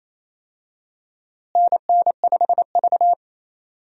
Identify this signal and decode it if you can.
Morse code, heard as a keyed tone, about 28 wpm, 700 Hz. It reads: NN5V
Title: USA Call Signs - 4 Characters - Repeated 3x - 28wpm